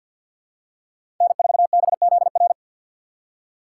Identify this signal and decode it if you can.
Morse code, heard as a keyed tone, about 54 wpm, 700 Hz. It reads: N4BZR
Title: USA Call Signs - 5 Characters - 8x Character Spacing - 50wpm